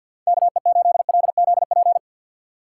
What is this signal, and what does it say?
Morse code, heard as a keyed tone, about 50 wpm, 700 Hz. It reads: KE8LZP
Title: USA Call Signs - 6 Characters - 4x Character Spacing - 50wpm